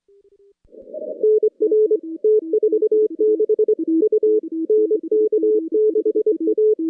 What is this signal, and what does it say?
Morse code, heard as a keyed tone, about 25 wpm, 430 Hz. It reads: NR T46 U NK6A
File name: contests / CW-SS-k5tr- -2200